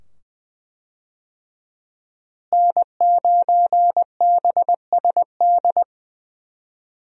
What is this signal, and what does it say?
Morse code, heard as a keyed tone, about 20 wpm, 700 Hz. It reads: N9BSD
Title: USA Call Signs - 5 Characters - Repeated 2x - 20wpm